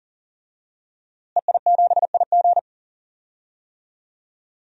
Morse code, heard as a keyed tone, about 40 wpm, 700 Hz. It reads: EI7IG
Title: DXCC-100 Call Signs - 5 Characters - Repeated 2x - 40wpm